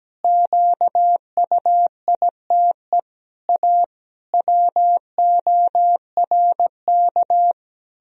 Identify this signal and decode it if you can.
Morse code, heard as a keyed tone, about 17 wpm, 700 Hz. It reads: QUITE A WORK
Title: Sentences from Top 1000 Words - Review Part 3 - 17wpm